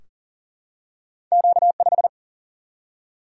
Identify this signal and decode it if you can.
Morse code, heard as a keyed tone, about 40 wpm, 700 Hz. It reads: Q5
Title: Single-Letter Single-Number 40wpm